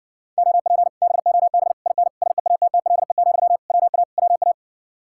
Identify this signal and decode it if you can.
Morse code, heard as a keyed tone, about 60 wpm, 700 Hz. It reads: OP BOB IN SEATTLE, WA WA